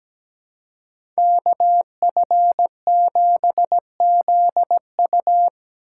Morse code, heard as a keyed tone, about 17 wpm, 700 Hz. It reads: KF7ZU
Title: USA Call Signs - 5 Characters - 2x Character Spacing - 17wpm